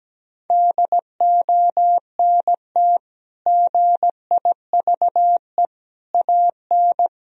Morse code, heard as a keyed tone, about 17 wpm, 700 Hz. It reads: DONT GIVE AN
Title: Sentences from Top 1000 Words - Review Part 3 - 17wpm